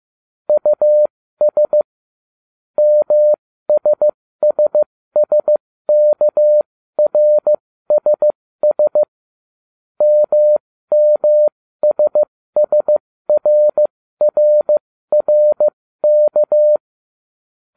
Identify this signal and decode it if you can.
Morse code, heard as a keyed tone, about 15 wpm, spaced as for 13 wpm, 600 Hz. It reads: US MSSSKRSS MMSSRRRK